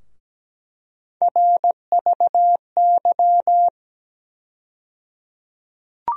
Morse code, heard as a keyed tone, about 17 wpm, 700 Hz. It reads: RVY
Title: USA Call Sign Postfixes - 3 Letters 17wpm